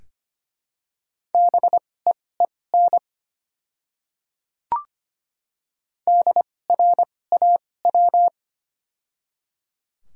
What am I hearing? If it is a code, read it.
Morse code, heard as a keyed tone, about 25 wpm, spaced as for 18 wpm, 700 Hz. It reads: BEEN DRAW
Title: 2x Character Spacing - Top 300 Words - 25wpm